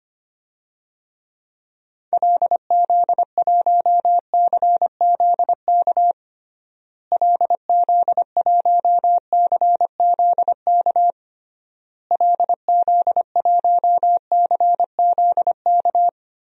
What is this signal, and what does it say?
Morse code, heard as a keyed tone, about 25 wpm, 700 Hz. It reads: LZ1CZK LZ1CZK LZ1CZK
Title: DXCC-100 Call Signs - 6 Characters - Repeated 3x - 25wpm